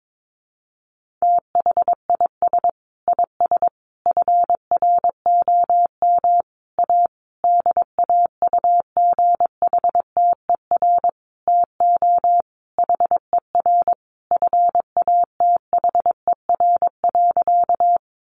Morse code, heard as a keyed tone, about 22 wpm, 700 Hz. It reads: THIS IS FROM A DAUGHTER TO HER FATHER.
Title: Sentences from Top 1500 Words 22wpm